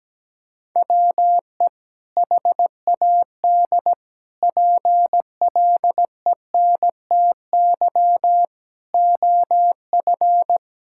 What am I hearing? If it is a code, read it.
Morse code, heard as a keyed tone, about 17 wpm, 700 Hz. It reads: WE HAD PLENTY OF
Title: Sentences from Top 1600 Words 17wpm